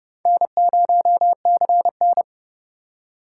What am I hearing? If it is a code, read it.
Morse code, heard as a keyed tone, about 29 wpm, 700 Hz. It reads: N0CN